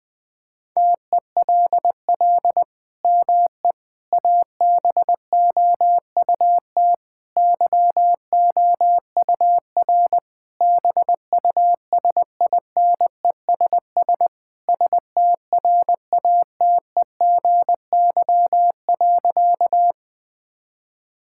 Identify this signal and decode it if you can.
Morse code, heard as a keyed tone, about 20 wpm, 700 Hz. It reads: TELL ME ABOUT YOUR BUSINESS STRATEGY.
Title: Sentences from Top 1000 Words - Review Part 5 - 20wpm